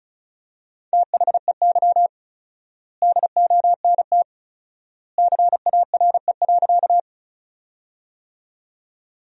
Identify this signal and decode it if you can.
Morse code, heard as a keyed tone, about 34 wpm, 700 Hz. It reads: THEY DONT CARE.
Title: 4x Word-Spacing - Sentences from Top 300 Words - 35wpm